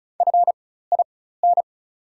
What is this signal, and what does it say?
Morse code, heard as a keyed tone, about 34 wpm, spaced as for 16 wpm, 690 Hz. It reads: FIN
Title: 4x Character Spacing - Top 500 Words - 35wpm